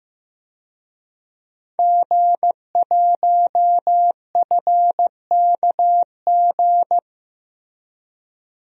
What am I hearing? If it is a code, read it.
Morse code, heard as a keyed tone, about 15 wpm, 700 Hz. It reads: G1FKG
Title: DXCC-100 Call Signs - 5 Characters - Repeated 2x - 15wpm